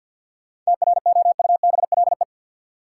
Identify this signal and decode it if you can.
Morse code, heard as a keyed tone, about 53 wpm, 700 Hz. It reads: TROUBLE
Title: Sentences from Top 1200 Words - New Word Drill - 50wpm